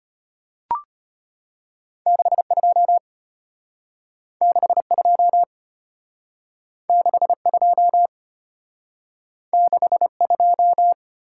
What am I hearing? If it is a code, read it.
Morse code, heard as a keyed tone, about 32 wpm, 700 Hz. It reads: 62 62 62 62